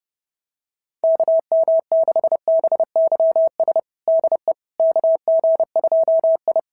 Morse code, heard as a keyed tone, about 30 wpm, 650 Hz. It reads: KM6BYS DE KG2I
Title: SOTA QSO Practice 30wpm